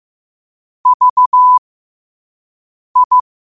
Morse code, heard as a keyed tone, about 15 wpm, spaced as for 5 wpm, 1000 Hz. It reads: VI
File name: qotdfiles / 5wpm